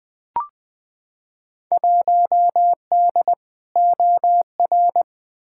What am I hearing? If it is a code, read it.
Morse code, heard as a keyed tone, about 20 wpm, 700 Hz. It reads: 1D OR
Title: Field Day Exchanges - 20wpm